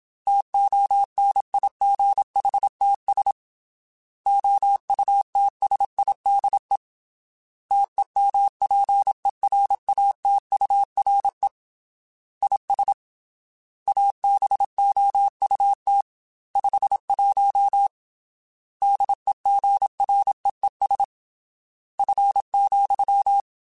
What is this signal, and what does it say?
Morse code, heard as a keyed tone, about 27 wpm, 800 Hz. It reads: TONIGHTS OUTSIDE TEMPERATURE IS ABOUT 51 DEGREES F,
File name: Weather Sets / Weather Sets-mp3s / Weather 4 21 @25